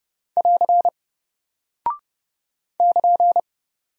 Morse code, heard as a keyed tone, about 30 wpm, 700 Hz. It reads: <AR> <KN>
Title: Single Character (Letter:Number:Punctuation:Prosign) Alt 30wpm